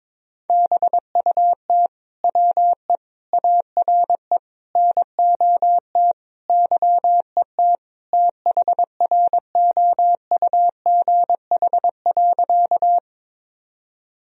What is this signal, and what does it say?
Morse code, heard as a keyed tone, about 22 wpm, 700 Hz. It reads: BUT WE ARE NOT YET THROUGH.
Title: Sentences from Top 1000 Words - Review Part 1 - 22wpm